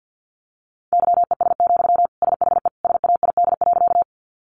Morse code, heard as a keyed tone, about 50 wpm, 700 Hz. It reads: YES, SHE SAID.
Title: Sentences from Top 500 Words 50wpm